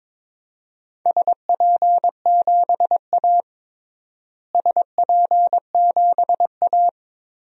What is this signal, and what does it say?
Morse code, heard as a keyed tone, about 21 wpm, 700 Hz. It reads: SP7A SP7A
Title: DXCC-100 Call Signs - 4 Characters - Repeated 3x - 22wpm